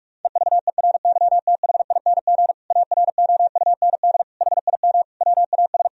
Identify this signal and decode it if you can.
Morse code, heard as a keyed tone, about 46 wpm, 700 Hz. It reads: EVERYTHING AROUND HIM WAS
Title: Sentences from Top 1000 Words - Review Part 3 - 45wpm